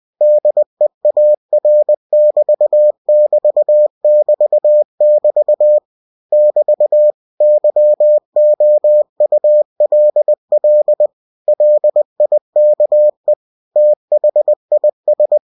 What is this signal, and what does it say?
Morse code, heard as a keyed tone, about 20 wpm, 600 Hz. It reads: DEAR<BT><BT><BT><BT> <BT> YOULL LIKE THIS